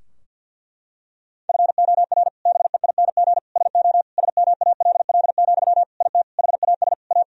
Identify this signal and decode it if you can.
Morse code, heard as a keyed tone, about 46 wpm, 700 Hz. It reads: FOR BEING SO SMALL, IT HAS A
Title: Sentences from Top 1000 Words - Review Part 3 - 50wpm